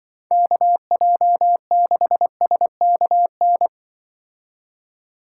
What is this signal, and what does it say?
Morse code, heard as a keyed tone, about 23 wpm, 700 Hz. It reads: KJ6SKN
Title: Speed Racing - USA 2x3 Call Signs 20wpm